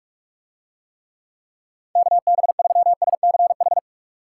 Morse code, heard as a keyed tone, about 44 wpm, 700 Hz. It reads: KB3SCH